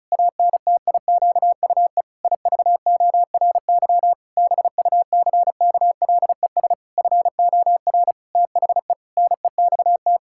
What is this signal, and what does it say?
Morse code, heard as a keyed tone, about 34 wpm, 700 Hz. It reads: ANTIQUE IVORY BUCKLES FOR THE NEXT
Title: Pangram Challenge v2 - 35wpm